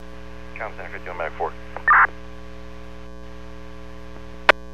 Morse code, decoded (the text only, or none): none